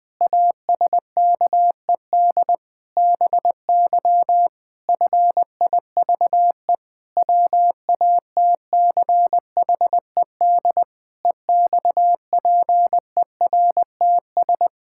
Morse code, heard as a keyed tone, about 21 wpm, 700 Hz. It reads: ASKED BY FIVE WATCHED EXPERTS
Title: Pangram Challenge v2 - 20wpm